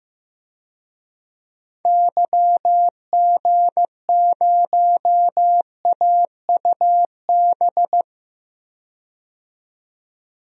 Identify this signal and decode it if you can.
Morse code, heard as a keyed tone, about 15 wpm, 700 Hz. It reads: YG0AUB
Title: DXCC-100 Call Signs - 6 Characters - 15wpm